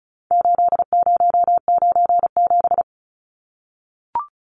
Morse code, heard as a keyed tone, about 34 wpm, 700 Hz. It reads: 8097
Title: Numbers - 4-Digit 35wpm